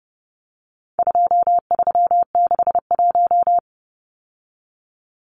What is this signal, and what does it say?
Morse code, heard as a keyed tone, about 30 wpm, 700 Hz. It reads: 2361
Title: Numbers - 4-Digit 30wpm